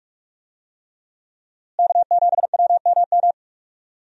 Morse code, heard as a keyed tone, about 43 wpm, 700 Hz. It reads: K7WMM
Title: USA Call Signs - 5 Characters - Repeated 3x - 45wpm